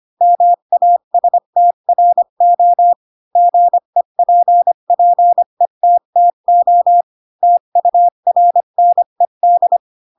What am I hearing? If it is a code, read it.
Morse code, heard as a keyed tone, about 25 wpm, 700 Hz. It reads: MASTRO GEPPETTO TURNED